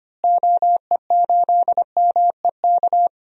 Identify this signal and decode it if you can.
Morse code, heard as a keyed tone, about 25 wpm, 700 Hz. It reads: OE8MEK
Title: DXCC-100 Call Signs - 6 Characters - Repeated 2x - 25wpm